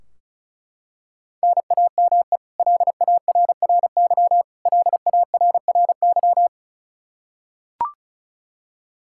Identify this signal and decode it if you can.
Morse code, heard as a keyed tone, about 36 wpm, 700 Hz. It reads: NAME LARRY LARRY
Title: Practice QSO - Exchange 1 - Beginning - 35wpm